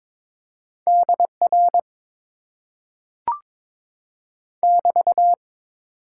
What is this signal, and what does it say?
Morse code, heard as a keyed tone, about 22 wpm, 700 Hz. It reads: DR <BT>
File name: Top 90 QSO Elements - 8x Character Spacing - 22wpm